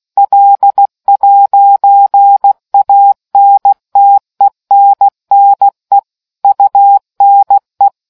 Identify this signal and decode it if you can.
Morse code, heard as a keyed tone, about 16 wpm, 800 Hz. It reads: L'ANTENNE UNE